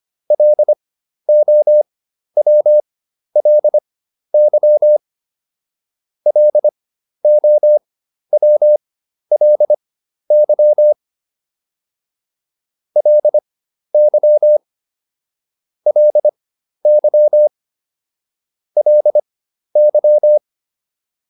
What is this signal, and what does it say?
Morse code, heard as a keyed tone, about 25 wpm, spaced as for 12 wpm, 600 Hz. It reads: LOWLY LOWLY LY LY LY